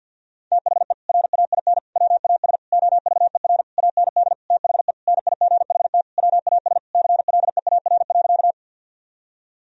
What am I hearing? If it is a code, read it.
Morse code, heard as a keyed tone, about 51 wpm, 700 Hz. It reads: THE RAIN WAS OVER AND THE NIGHT WAS CLEAR.
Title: Sentences from Top 1000 Words - Review Part 1 - 50wpm